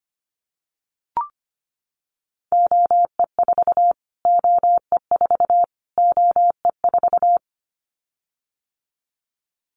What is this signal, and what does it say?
Morse code, heard as a keyed tone, about 25 wpm, 700 Hz. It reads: OE4 OE4 OE4
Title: Callsign Prefixes for Top 25 DXCC - Repeated 3x - 25wpm